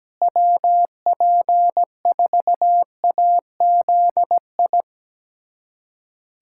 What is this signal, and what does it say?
Morse code, heard as a keyed tone, about 17 wpm, 700 Hz. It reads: WP4AZI
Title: USA Callsign DE Callsign 17wpm